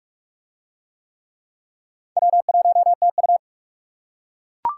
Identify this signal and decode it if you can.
Morse code, heard as a keyed tone, about 39 wpm, 700 Hz. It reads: W1TU